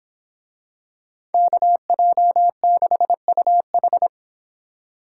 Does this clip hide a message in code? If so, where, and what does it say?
Yes: Morse code from 1.3 to 4.1 s, heard as a keyed tone, about 24 wpm, 700 Hz. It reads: KJ6UH